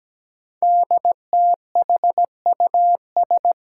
Morse code, heard as a keyed tone, about 17 wpm, 700 Hz. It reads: DTHUS
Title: Five Challenging Letter-Number Combinations 17wpm